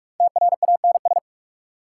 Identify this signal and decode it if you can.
Morse code, heard as a keyed tone, about 46 wpm, 690 Hz. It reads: TRANS
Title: Prefixes and Suffixes 45wpm